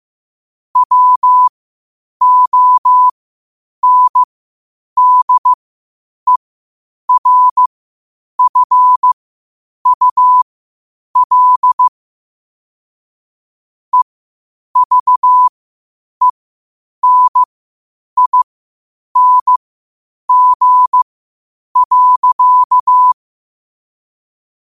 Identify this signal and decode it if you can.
Morse code, heard as a keyed tone, about 15 wpm, spaced as for 8 wpm, 1000 Hz. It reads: WONDERFUL EVENING.
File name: qotdfiles / 8wpm